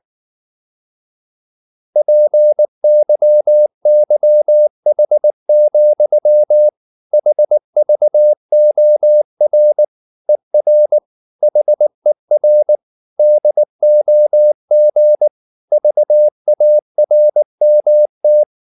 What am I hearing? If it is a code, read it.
Morse code, heard as a keyed tone, about 19 wpm, 600 Hz. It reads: PYYH, HVOR ER HER DOG VARMT
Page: Ferie 19 wpm